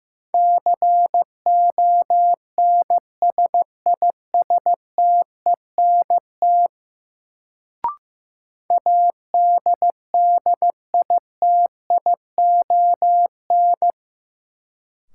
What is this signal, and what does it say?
Morse code, heard as a keyed tone, about 15 wpm, 700 Hz. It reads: CONSISTENT ADDITION
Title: Sentences from Top 1500 Words - New Word Drill Alt - 15wpm